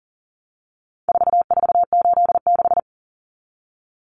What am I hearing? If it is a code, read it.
Morse code, heard as a keyed tone, about 40 wpm, 700 Hz. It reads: 4486
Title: Numbers - 4-Digit 40wpm